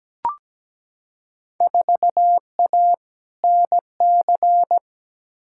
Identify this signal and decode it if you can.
Morse code, heard as a keyed tone, about 17 wpm, 700 Hz. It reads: 4A NC
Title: Field Day Exchanges - 17wpm